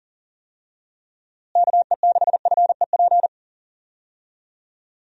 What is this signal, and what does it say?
Morse code, heard as a keyed tone, about 37 wpm, 700 Hz. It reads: KE6FEP
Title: USA Call Signs - 6 Characters - Repeated 2x - 40wpm